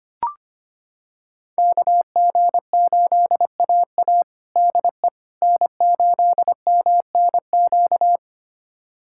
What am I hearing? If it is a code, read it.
Morse code, heard as a keyed tone, about 25 wpm, 700 Hz. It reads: KG8AA DE N8MNQ
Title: USA Callsign DE Callsign - 5 Characters - 25wpm